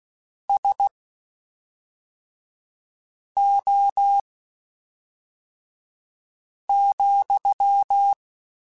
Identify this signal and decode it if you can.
Morse code, heard as a keyed tone, about 16 wpm, spaced as for 3 wpm, 800 Hz. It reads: SO,
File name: QSOs / QSOs-mp3s / WD0BGP 1@5